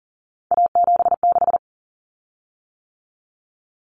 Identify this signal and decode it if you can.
Morse code, heard as a keyed tone, about 40 wpm, 700 Hz. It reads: A76
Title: Common Cut Numbers - 3-Digit 40wpm